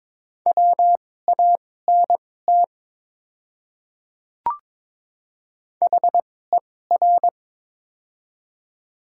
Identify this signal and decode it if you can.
Morse code, heard as a keyed tone, about 22 wpm, spaced as for 16 wpm, 700 Hz. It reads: WANT HER
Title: 2x Character Spacing - Top 200 Words - 22wpm